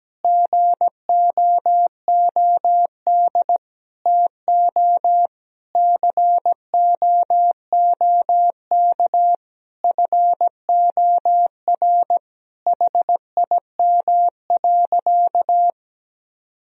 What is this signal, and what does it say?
Morse code, heard as a keyed tone, about 17 wpm, 700 Hz. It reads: GOOD TO COOK FOR HIM.
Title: Sentences from Top 1000 Words - Review Part 4 - 17wpm